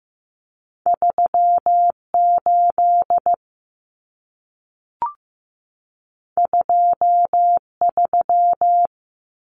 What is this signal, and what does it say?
Morse code, heard as a keyed tone, about 15 wpm, 700 Hz. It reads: 38 23
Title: Common Cut Numbers - 2-Digit 15wpm